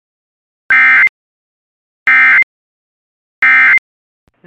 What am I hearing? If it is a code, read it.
Emergency Alert System (SAME) tones, heard as harsh buzzing bursts. It is heard from 0.7 to 3.8 s.